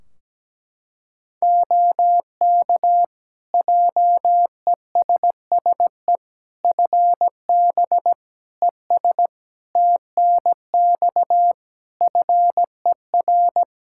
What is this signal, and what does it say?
Morse code, heard as a keyed tone, about 17 wpm, 700 Hz. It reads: OK JESSE FB ES TNX FER
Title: Practice QSO - Exchange 2 - Info - 17wpm